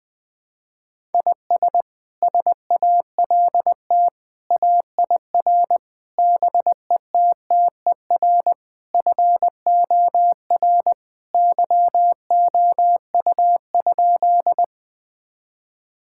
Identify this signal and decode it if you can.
Morse code, heard as a keyed tone, about 21 wpm, 700 Hz. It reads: IS SALT AIR BETTER FOR YOU?
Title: Sentences from Top 1000 Words - Review Part 5 - 20wpm